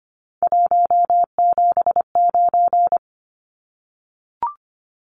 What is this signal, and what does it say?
Morse code, heard as a keyed tone, about 25 wpm, 700 Hz. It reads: 179
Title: Numbers - 3-Digit 25wpm